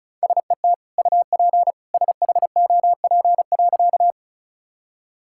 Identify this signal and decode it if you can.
Morse code, heard as a keyed tone, about 34 wpm, 700 Hz. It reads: SET UP SHOP.
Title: Sentences from Top 1600 Words 35wpm